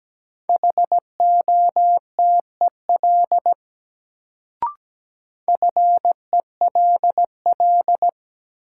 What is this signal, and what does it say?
Morse code, heard as a keyed tone, about 17 wpm, 700 Hz. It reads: HOTEL FELL
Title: Sentences from Top 1600 Words - New Word Drill Alt - 17wpm